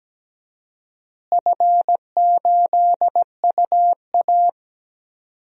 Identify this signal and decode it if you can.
Morse code, heard as a keyed tone, about 17 wpm, 700 Hz. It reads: F8UA
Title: DXCC-100 Call Signs - 4 Characters - Repeated 3x - 17wpm